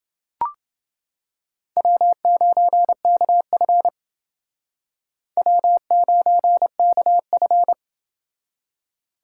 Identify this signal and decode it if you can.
Morse code, heard as a keyed tone, about 29 wpm, 700 Hz. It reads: W9KF W9KF